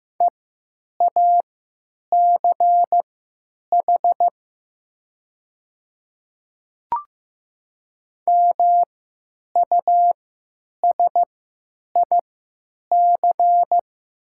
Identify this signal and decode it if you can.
Morse code, heard as a keyed tone, about 15 wpm, spaced as for 9 wpm, 700 Hz. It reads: EACH MUSIC
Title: 3x Character Spacing - Top 300 Words - 15wpm